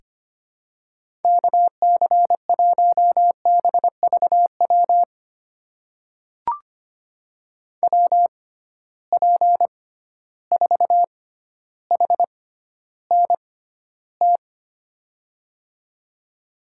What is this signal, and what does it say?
Morse code, heard as a keyed tone, about 25 wpm, spaced as for 13 wpm, 700 Hz. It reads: KC1BVW WP4HNT
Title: USA Call Signs - 6 Characters - 6x Character Spacing - 25wpm